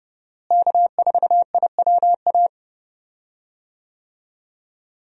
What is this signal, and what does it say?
Morse code, heard as a keyed tone, about 30 wpm, 700 Hz. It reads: K4IWA
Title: USA Call Signs - 5 Characters - Repeated 3x - 30wpm